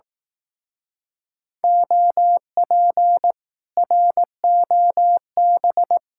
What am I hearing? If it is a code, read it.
Morse code, heard as a keyed tone, about 18 wpm, 700 Hz. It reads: OP ROB
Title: Speed-Racing Common QSO Phrases Encore - 15wpm